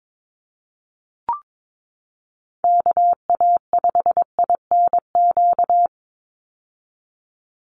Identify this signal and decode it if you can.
Morse code, heard as a keyed tone, about 22 wpm, 700 Hz. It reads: KA5INQ
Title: USA Call Signs 22wpm